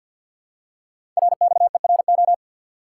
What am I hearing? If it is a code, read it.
Morse code, heard as a keyed tone, about 50 wpm, 700 Hz. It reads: RXERO
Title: Code Groups - 5 Letters - 50wpm